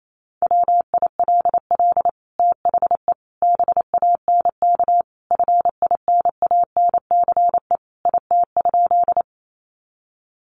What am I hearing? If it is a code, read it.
Morse code, heard as a keyed tone, about 28 wpm, 700 Hz. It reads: WILL THE BANK FINANCE IT?
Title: Sentences from Top 1500 Words 28wpm